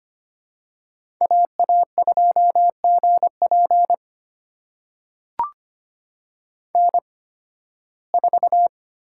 Morse code, heard as a keyed tone, about 25 wpm, spaced as for 16 wpm, 700 Hz. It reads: AA2GP N4
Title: USA Call Signs - 5 Characters - 8x Character Spacing - 25wpm